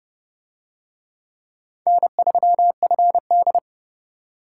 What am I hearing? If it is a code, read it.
Morse code, heard as a keyed tone, about 29 wpm, 700 Hz. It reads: N3FD